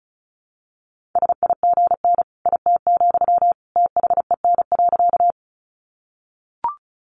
Morse code, heard as a keyed tone, about 34 wpm, 700 Hz. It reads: SIGN IT, THEN.
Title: Sentences from Top 1000 Words 35wpm